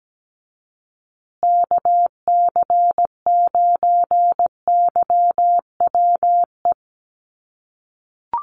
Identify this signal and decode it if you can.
Morse code, heard as a keyed tone, about 17 wpm, 700 Hz. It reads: KC9YWE
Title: USA Call Signs Repeated 3x 17wpm